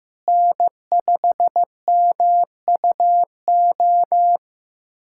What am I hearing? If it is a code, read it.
Morse code, heard as a keyed tone, about 15 wpm, 700 Hz. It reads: N5MUO
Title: USA Callsign DE Callsign 15wpm